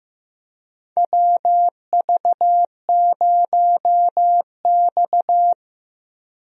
Morse code, heard as a keyed tone, about 15 wpm, 700 Hz. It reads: WV0X
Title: USA Call Signs - 4 Characters - Repeated 2x - 15wpm